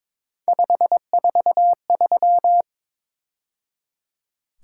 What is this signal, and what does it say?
Morse code, heard as a keyed tone, about 26 wpm, 700 Hz. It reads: 543
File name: Speed-Racing - Three Letter-Number Combinations - 22wpm